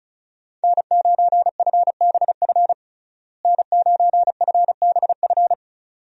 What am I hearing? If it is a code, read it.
Morse code, heard as a keyed tone, about 36 wpm, 700 Hz. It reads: N9FBF N9FBF
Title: USA Call Signs - 5 Characters - Repeated 2x - 35wpm